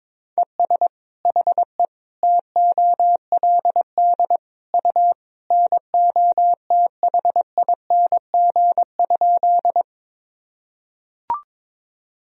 Morse code, heard as a keyed tone, about 22 wpm, 700 Hz. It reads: ES HE TOLD U NOTHING?
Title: Sentences from Top 500 Words and Abbreviations 22wpm